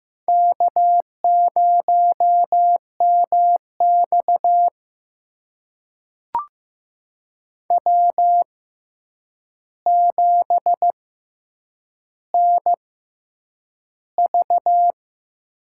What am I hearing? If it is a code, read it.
Morse code, heard as a keyed tone, about 15 wpm, spaced as for 8 wpm, 700 Hz. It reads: K0MX W7NV
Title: USA Call Signs - 6x Character Spacing - 15wpm